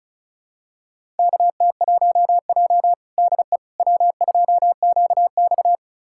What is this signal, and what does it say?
Morse code, heard as a keyed tone, about 34 wpm, 700 Hz. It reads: KT1J DE W2QX
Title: USA Callsign DE Callsign - 4 Characters - 35wpm